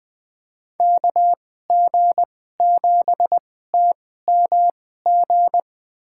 Morse code, heard as a keyed tone, about 21 wpm, spaced as for 15 wpm, 700 Hz. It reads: KG7TMG
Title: USA Call Signs - 6 Characters - 2x Character Spacing - 20wpm